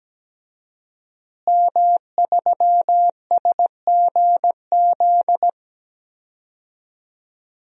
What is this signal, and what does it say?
Morse code, heard as a keyed tone, about 17 wpm, 700 Hz. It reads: M3SGZ
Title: DXCC-100 Call Signs - 5 Characters - Repeated 2x - 17wpm